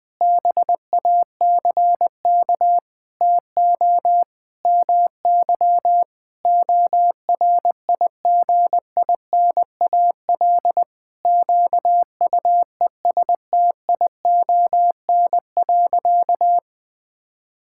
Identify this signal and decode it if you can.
Morse code, heard as a keyed tone, about 21 wpm, 700 Hz. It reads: BACK TO MY ORIGINAL QUESTION.
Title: Sentences from Top 1000 Words - Review Part 6 - 20wpm